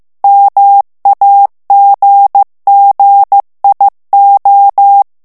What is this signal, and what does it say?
Morse code, heard as a keyed tone, about 15 wpm, 800 Hz. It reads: MAGGIO
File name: morse mp3 / ITALIEN texte clair 1 vitesse 14 mots minute